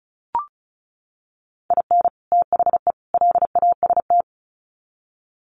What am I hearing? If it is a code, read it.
Morse code, heard as a keyed tone, about 34 wpm, 700 Hz. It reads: IN THE LAST
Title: Sets of 3 Words 35wpm